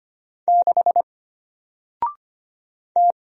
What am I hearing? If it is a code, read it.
Morse code, heard as a keyed tone, about 25 wpm, 700 Hz. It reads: 6 T
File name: Single-Letter-Number-25wpm